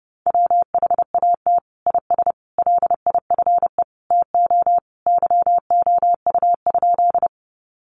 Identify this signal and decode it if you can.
Morse code, heard as a keyed tone, about 30 wpm, 700 Hz. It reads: WHAT IS LIFE TO YOU?
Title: Sentences from Top 300 Words 30wpm